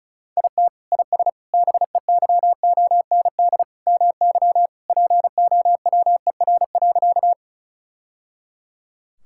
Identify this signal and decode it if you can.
Morse code, heard as a keyed tone, about 36 wpm, 700 Hz. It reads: IT IS BEYOND MY POWER.
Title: Sentences from Top 1000 Words - Review Part 6 - 35wpm